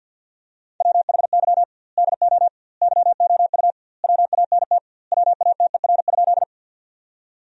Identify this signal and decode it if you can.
Morse code, heard as a keyed tone, about 50 wpm, 700 Hz. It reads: WHY DO YOU WANT WATER?
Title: 2x Word-Spacing - Sentences from Top 200 Words - 50wpm